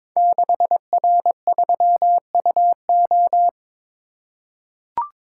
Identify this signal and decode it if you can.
Morse code, heard as a keyed tone, about 22 wpm, 700 Hz. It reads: 6R3UO
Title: Code Groups - 5 Letter:Numbers - 22wpm